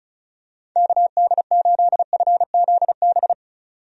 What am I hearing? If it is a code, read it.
Morse code, heard as a keyed tone, about 36 wpm, 700 Hz. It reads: KD8FZB
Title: USA Call Signs - 6 Characters - 35wpm